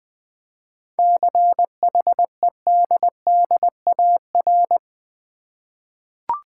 Morse code, heard as a keyed tone, about 20 wpm, 700 Hz. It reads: CHEDDAR
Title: Easy Words with Hints - 20wpm